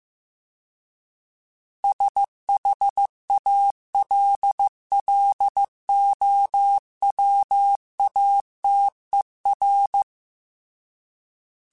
Morse code, heard as a keyed tone, about 15 wpm, 800 Hz. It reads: SHALLOWATER
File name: fromSMs / FromSMs mp3s / Cities@13